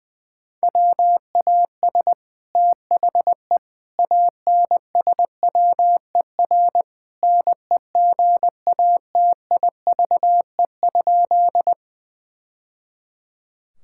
Morse code, heard as a keyed tone, about 20 wpm, 700 Hz. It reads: WAS THE ANSWER NEGATIVE?
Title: Sentences from Top 1000 Words - Review Part 6 - 20wpm